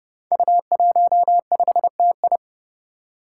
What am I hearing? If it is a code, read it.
Morse code, heard as a keyed tone, about 30 wpm, 700 Hz. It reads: U15TI
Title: Five Challenging Letter-Number Combinations 30wpm